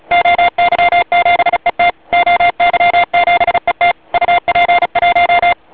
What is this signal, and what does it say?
Morse code, heard as a keyed tone, about 35 wpm, 700 Hz, with noise. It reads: OY7ET OY7ET UP1
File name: oy 2003 / real-audio / 12m-oy7et-cw-3min